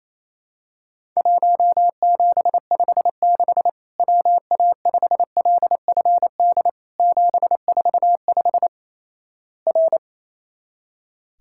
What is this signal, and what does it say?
Morse code, heard as a keyed tone, about 28 wpm, 700 Hz. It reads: 1756 WA5LFD 745 R
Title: Worked All Europe DX Contest - QTC Exchange - 28wpm